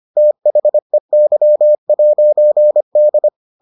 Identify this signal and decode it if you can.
Morse code, heard as a keyed tone, about 25 wpm, 600 Hz. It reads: THEY'D